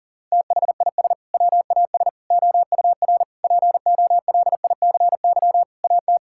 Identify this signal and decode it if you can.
Morse code, heard as a keyed tone, about 40 wpm, 700 Hz. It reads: THIS WAS OUR POLICY AT
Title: Sentences from Top 1000 Words - Review Part 5 - 40wpm